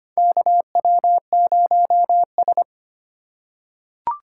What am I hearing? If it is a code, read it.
Morse code, heard as a keyed tone, about 25 wpm, 700 Hz. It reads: KW0S
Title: USA Call Signs - 4 Characters - 2x Character Spacing - 25wpm